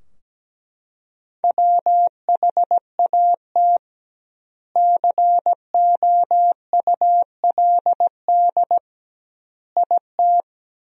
Morse code, heard as a keyed tone, about 17 wpm, 700 Hz. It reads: WHAT COULD IT
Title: 2x Word-Spacing - Sentences from Top 100 Words - 17wpm